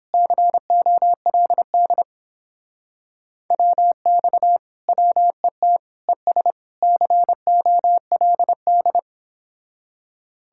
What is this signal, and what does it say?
Morse code, heard as a keyed tone, about 31 wpm, 700 Hz. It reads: COLD WX WET ES COLD